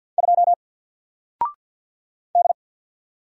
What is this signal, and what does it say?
Morse code, heard as a keyed tone, about 48 wpm, 700 Hz. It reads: 2 D
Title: Single Letter-Number Alt 50wpm